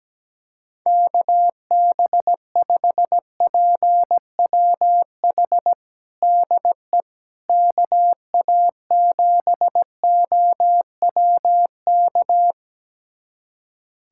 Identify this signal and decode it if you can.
Morse code, heard as a keyed tone, about 17 wpm, 700 Hz. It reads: KB5PWH DE KA7OWK
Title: USA Callsign DE Callsign 17wpm